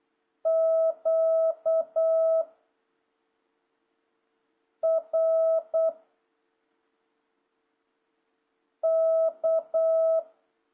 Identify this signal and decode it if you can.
Morse code, heard as a keyed tone, about 8 wpm, 650 Hz. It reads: Q R K
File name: jh8cbh / 2026cw